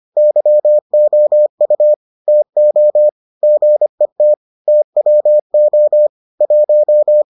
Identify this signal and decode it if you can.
Morse code, heard as a keyed tone, about 25 wpm, 600 Hz. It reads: YOU TO GET TWO 1